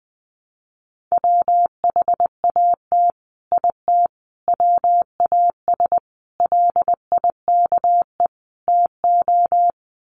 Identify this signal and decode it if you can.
Morse code, heard as a keyed tone, about 20 wpm, 700 Hz. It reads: WHAT IT WAS LIKE TO
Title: Sets of 5 Words Encore 20wpm